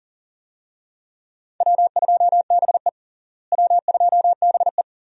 Morse code, heard as a keyed tone, about 40 wpm, 700 Hz. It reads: W2BE W2BE
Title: USA Call Signs - 4 Characters - Repeated 2x - 40wpm